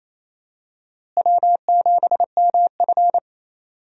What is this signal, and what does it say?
Morse code, heard as a keyed tone, about 28 wpm, 700 Hz. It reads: W7MF
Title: USA Call Signs - 4 Characters - Repeated 2x - 28wpm